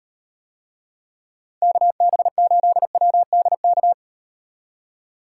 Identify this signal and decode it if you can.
Morse code, heard as a keyed tone, about 38 wpm, 700 Hz. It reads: KB8WDK